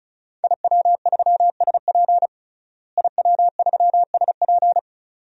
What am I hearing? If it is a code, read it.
Morse code, heard as a keyed tone, about 36 wpm, 700 Hz. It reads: IW3SP IW3SP
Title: DXCC-100 Call Signs - 5 Characters - Repeated 2x - 35wpm